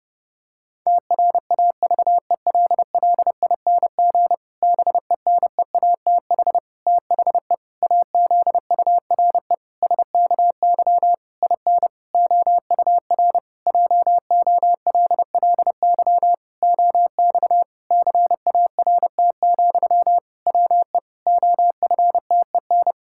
Morse code, heard as a keyed tone, about 30 wpm, 700 Hz. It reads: TRAVELLING BENEATH THE AZURE SKY IN OUR JOLLY OX CART, WE OFTEN
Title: Pangram Challenge v2 - 30wpm